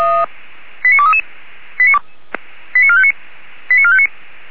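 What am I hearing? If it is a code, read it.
Five-tone selective call (ZVEI), heard as quick runs of five tones, tones of about 70 ms. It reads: 89233 89459 89459
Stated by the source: ZVEI five-tone call 89241, received on SDR.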